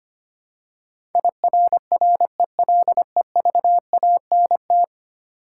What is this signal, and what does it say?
Morse code, heard as a keyed tone, about 25 wpm, 700 Hz. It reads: IRRELEVANT
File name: Ten Letter Words - 25wpm